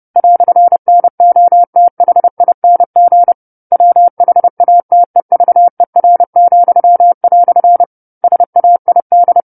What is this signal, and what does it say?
Morse code, heard as a keyed tone, about 30 wpm, 700 Hz. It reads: "NOTHING WHATEVER," SAID
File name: 10-Chapter-10 30 WPM